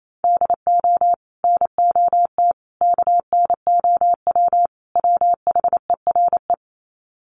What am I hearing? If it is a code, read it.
Morse code, heard as a keyed tone, about 28 wpm, 700 Hz. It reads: DO NOT KNOW WHERE
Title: Sets of 4 Words 28wpm